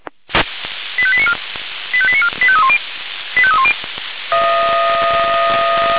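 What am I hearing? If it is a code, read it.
Five-tone selective call (ZVEI), heard as quick runs of five tones, tones of about 70 ms. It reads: 85894 85894 85310 85310
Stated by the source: ZVEI five-tone selective call received on an SDR, recorded by Feuerwehr Orlamünde fire brigade.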